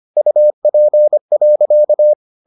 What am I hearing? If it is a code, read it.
Morse code, heard as a keyed tone, about 25 wpm, 600 Hz. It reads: UP.